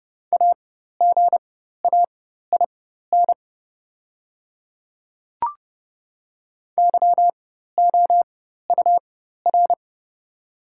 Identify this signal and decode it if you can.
Morse code, heard as a keyed tone, about 30 wpm, spaced as for 14 wpm, 700 Hz. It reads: AGAIN YOUR
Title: 4x Character Spacing - Top 200 Words - 30wpm